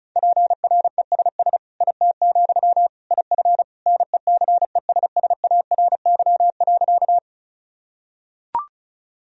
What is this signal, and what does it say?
Morse code, heard as a keyed tone, about 36 wpm, 700 Hz. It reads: PRESS IT, IF NECESSARY.
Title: Sentences from Top 1000 Words - Review Part 1 - 35wpm